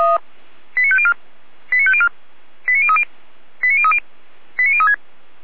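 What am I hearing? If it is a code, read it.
Five-tone selective call (ZVEI), heard as quick runs of five tones, tones of about 70 ms. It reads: 89504 89504 89030 89030 89036
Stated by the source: ZVEI five-tone selective call received by SDR.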